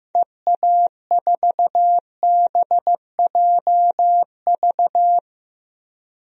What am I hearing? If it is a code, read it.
Morse code, heard as a keyed tone, about 15 wpm, 700 Hz. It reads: EA4BJV
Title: DXCC-100 Call Signs - 6 Characters - Repeated 3x - 15wpm